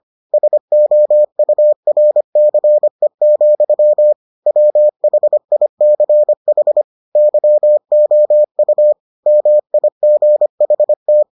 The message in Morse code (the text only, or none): SOURCE, WHICH YOU MIGHT